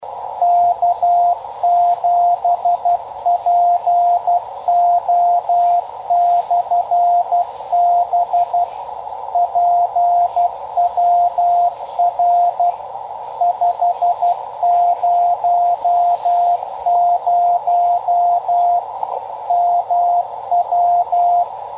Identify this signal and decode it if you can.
Morse code, heard as a keyed tone, about 12 wpm, 700 Hz, with radio noise. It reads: K7PO/B PWR 500 MW
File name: Half Watt 10 Meter Beacon